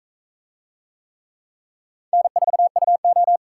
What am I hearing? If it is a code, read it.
Morse code, heard as a keyed tone, about 41 wpm, 700 Hz. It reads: N4UO